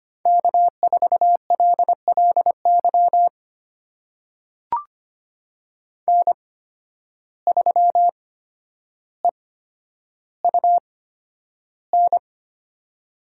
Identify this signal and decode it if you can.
Morse code, heard as a keyed tone, about 25 wpm, spaced as for 11 wpm, 700 Hz. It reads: K4LLY N3EUN
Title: USA Call Signs - 5 Characters - 8x Character Spacing - 25wpm